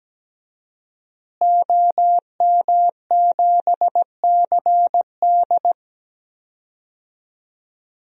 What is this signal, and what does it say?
Morse code, heard as a keyed tone, about 17 wpm, 700 Hz. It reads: OM7CD
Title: DXCC-100 Call Signs - 5 Characters - Repeated 3x - 17wpm